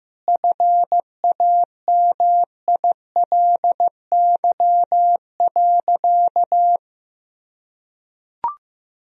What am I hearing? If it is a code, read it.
Morse code, heard as a keyed tone, about 15 wpm, 700 Hz. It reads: FAMILY.
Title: Sentences from Top 1000 Words - Review Part 2 - 15wpm